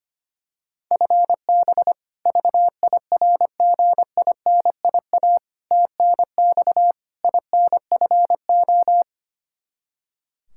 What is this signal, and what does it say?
Morse code, heard as a keyed tone, about 25 wpm, 700 Hz. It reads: FB VIRGINIA TNX INFO
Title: K1USN SST QSO Practice 25wpm